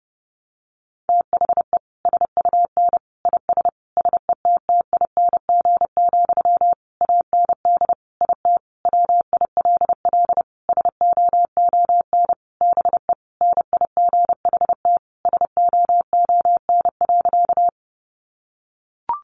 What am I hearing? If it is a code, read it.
Morse code, heard as a keyed tone, about 30 wpm, 700 Hz. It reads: THE SUN IS SETTING, AND IT WILL SOON BE NIGHT SOON.
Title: Sentences from Top 1300 Words 30wpm